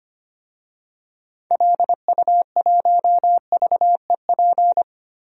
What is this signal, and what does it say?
Morse code, heard as a keyed tone, about 25 wpm, 700 Hz. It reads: LU1VEP
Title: DXCC-100 Call Signs - 6 Characters - Repeated 2x - 25wpm